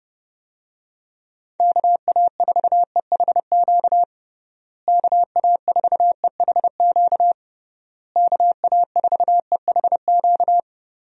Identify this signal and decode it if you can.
Morse code, heard as a keyed tone, about 30 wpm, 700 Hz. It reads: KA4EHQ KA4EHQ KA4EHQ
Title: USA Call Signs - 6 Characters - Repeated 3x - 30wpm